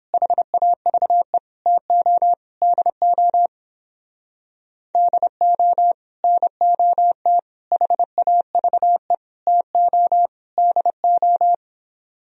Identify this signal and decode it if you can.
Morse code, heard as a keyed tone, about 29 wpm, 700 Hz. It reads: HAVE TO DO DO NOT HAVE TO DO